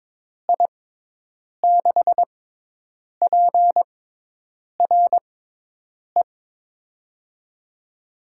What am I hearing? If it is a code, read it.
Morse code, heard as a keyed tone, about 22 wpm, spaced as for 8 wpm, 700 Hz. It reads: I6PRE
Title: USA Call Signs - 6 Characters - 6x Character Spacing - 22wpm